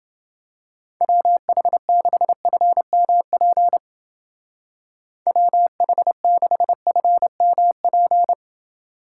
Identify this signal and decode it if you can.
Morse code, heard as a keyed tone, about 31 wpm, 700 Hz. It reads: WH6FMP WH6FMP